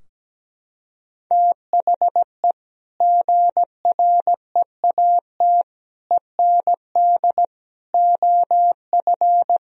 Morse code, heard as a keyed tone, about 17 wpm, 700 Hz. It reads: THE GREAT END OF
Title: Sentences from Top 1000 Words - Review Part 4 - 17wpm